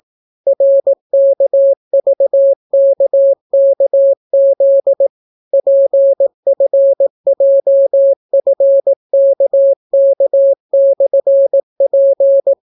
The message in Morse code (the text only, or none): RKVKKZ PFJFKK/P